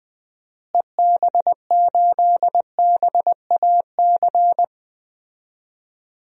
Morse code, heard as a keyed tone, about 20 wpm, 700 Hz. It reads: EB8BAC
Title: DXCC-100 Call Signs - 6 Characters - Repeated 2x - 20wpm